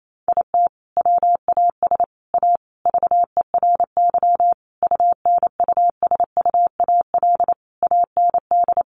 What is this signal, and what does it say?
Morse code, heard as a keyed tone, about 28 wpm, 700 Hz. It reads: IT WAS A VERY UNUSUAL AND
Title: Sentences from Top 1400 Words 28wpm